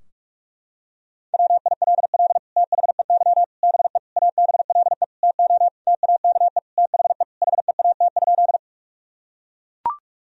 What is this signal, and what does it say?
Morse code, heard as a keyed tone, about 41 wpm, 700 Hz. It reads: WILL THEY BE ABLE TO TAKE THE HEAT?
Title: Sentences from Top 1000 Words - Review Part 3 - 45wpm